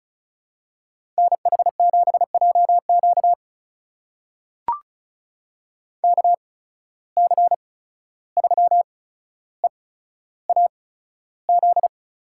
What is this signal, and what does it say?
Morse code, heard as a keyed tone, about 36 wpm, spaced as for 14 wpm, 700 Hz. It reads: NH7JQ KC3EAZ
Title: USA Call Signs - 8x Character Spacing - 35wpm